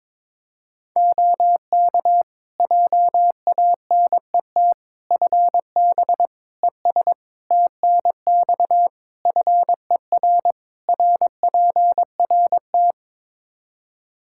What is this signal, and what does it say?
Morse code, heard as a keyed tone, about 22 wpm, 700 Hz. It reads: OK JANET FB ES TNX FER RPRT
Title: Practice QSO - Exchange 2 - Info - 22wpm